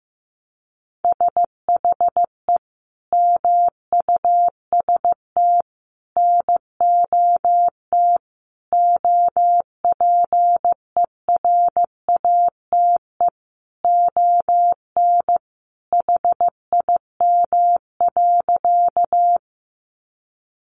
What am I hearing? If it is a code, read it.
Morse code, heard as a keyed tone, about 15 wpm, 700 Hz. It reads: SHE MUST NOT OPERATE ON HIM.
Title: Sentences from Top 1500 Words 15wpm